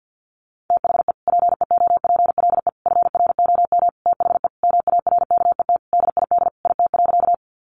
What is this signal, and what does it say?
Morse code, heard as a keyed tone, about 50 wpm, 700 Hz. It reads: THE PEOPLE FROM THE MARKET DID IT.
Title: Sentences from Top 600 Words 50wpm